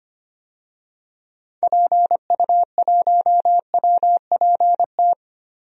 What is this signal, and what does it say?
Morse code, heard as a keyed tone, about 25 wpm, 700 Hz. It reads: PU1WPT
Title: DXCC-100 Call Signs - 6 Characters - 25wpm